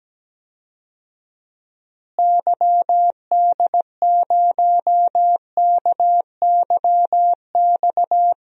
Morse code, heard as a keyed tone, about 17 wpm, 700 Hz. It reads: YD0KYX
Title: DXCC-100 Call Signs - 6 Characters - Repeated 2x - 17wpm